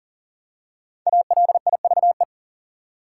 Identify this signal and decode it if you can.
Morse code, heard as a keyed tone, about 40 wpm, 700 Hz. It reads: ALIVE
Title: Five Letter Words - 40wpm